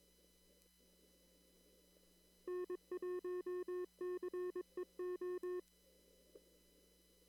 Morse code, heard as a keyed tone, about 22 wpm, 360 Hz. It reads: N1CEO
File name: CW / kt5j wpx cw-op-k5pi